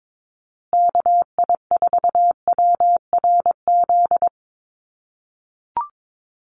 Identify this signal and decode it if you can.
Morse code, heard as a keyed tone, about 22 wpm, 690 Hz. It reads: KI4WRZ
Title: USA Call Signs Repeated 3x 22wpm